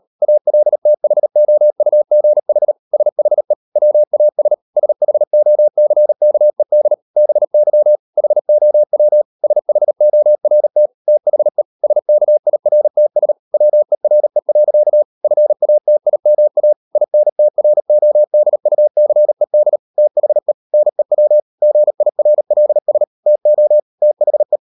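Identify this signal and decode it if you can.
Morse code, heard as a keyed tone, about 38 wpm, 600 Hz. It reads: ALTHOUGH SHE WAS SHOCKED BY HOW SHORT THE SKIRTS WERE. FATIMA INTRODUCED THE NEW GIRLS TO THE
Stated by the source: Morse code practice at 38 WPM, sending Ken Follett's novel Never in English, from CW med Gnister.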